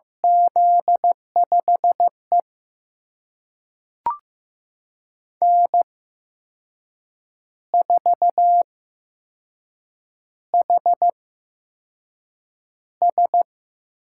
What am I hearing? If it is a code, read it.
Morse code, heard as a keyed tone, about 15 wpm, spaced as for 6 wpm, 700 Hz. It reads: Z5E N4HS
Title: USA Call Signs - 4 Characters - 8x Character Spacing - 15wpm